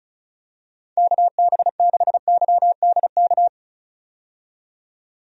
Morse code, heard as a keyed tone, about 34 wpm, 700 Hz. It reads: KB6YDK
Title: USA Call Signs - 6 Characters - 35wpm